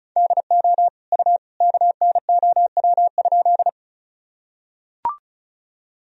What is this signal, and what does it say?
Morse code, heard as a keyed tone, about 36 wpm, 700 Hz. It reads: DO U KNOW?
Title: Sentences from Top 100 Words and Abbreviations 35wpm